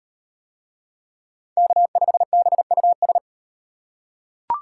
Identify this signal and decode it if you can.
Morse code, heard as a keyed tone, about 37 wpm, 700 Hz. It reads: K5BUS